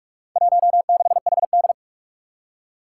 Morse code, heard as a keyed tone, about 46 wpm, 700 Hz. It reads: 16HD
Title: Four Challenging Letter-Number Combinations 45wpm